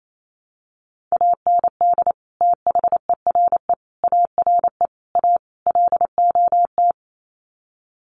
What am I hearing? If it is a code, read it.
Morse code, heard as a keyed tone, about 28 wpm, 700 Hz. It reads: AND THERE ARE A LOT
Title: Sets of 5 Words Encore2 28wpm